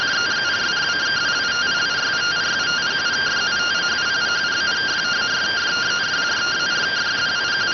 Two-tone teleprinter signal, about 170 Hz shift.